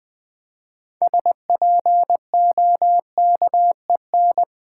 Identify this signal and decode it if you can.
Morse code, heard as a keyed tone, about 20 wpm, 700 Hz. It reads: SPOKEN
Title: 6 Letter Words - First 3 Letters Given - 20wpm